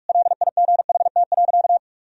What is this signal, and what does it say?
Morse code, heard as a keyed tone, about 46 wpm, 700 Hz. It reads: LIGHT.
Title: Sentences from Top 1000 Words - Review Part 1 - 45wpm